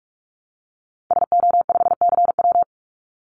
Morse code, heard as a keyed tone, about 44 wpm, 700 Hz. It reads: SO5CW
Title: Call Signs - 45wpm